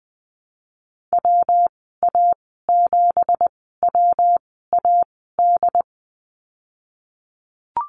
Morse code, heard as a keyed tone, about 20 wpm, spaced as for 15 wpm, 700 Hz. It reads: WA7WAD
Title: 173 Lesson 28g - Number 9 - Call Signs 20wpm